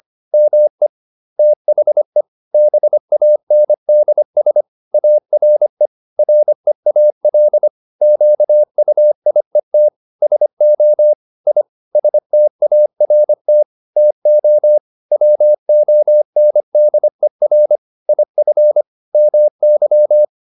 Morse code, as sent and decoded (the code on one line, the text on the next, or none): -- . / - .... / -... .- -. -.. ... / .- .-. / .-. .- .-.. / --.- ..- .. - / ... --- / .. / ... - .- .-. - / - --- / .-- --- -. -.. .-. / .. ..-. / -- -.--
ME THE BANDS ARE REAL QUIET SO I START TO WONDER IF MY